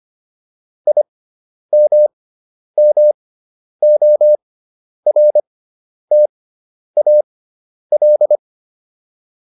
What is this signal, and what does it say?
Morse code, heard as a keyed tone, about 25 wpm, spaced as for 10 wpm, 600 Hz. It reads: IMMORTAL